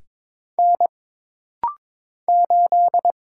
Morse code, heard as a keyed tone, about 22 wpm, 700 Hz. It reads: N 8
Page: This rapid-fire timing is geared to push your ICR (Instant Character Recognition) skills.